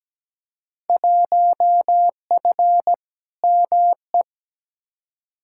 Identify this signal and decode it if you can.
Morse code, heard as a keyed tone, about 17 wpm, 700 Hz. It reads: 1F ME
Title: Field Day Exchanges Encore - 17wpm